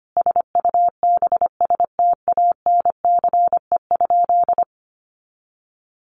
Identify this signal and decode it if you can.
Morse code, heard as a keyed tone, about 25 wpm, 700 Hz. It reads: SUBSTANCE?
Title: Sentences from Top 1500 Words 25wpm